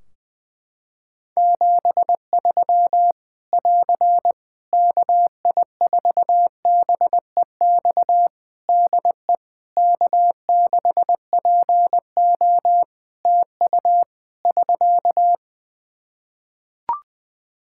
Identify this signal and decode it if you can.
Morse code, heard as a keyed tone, about 20 wpm, 700 Hz. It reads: 73 <AR> KI4BEX DE K6PO TU <SK>
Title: Practice QSO - Exchange 4 - Ending - 20wpm